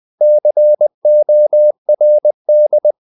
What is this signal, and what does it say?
Morse code, heard as a keyed tone, about 20 wpm, 600 Hz. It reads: CORD
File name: the old man and the sea / the old man and the sea part 1 20wpm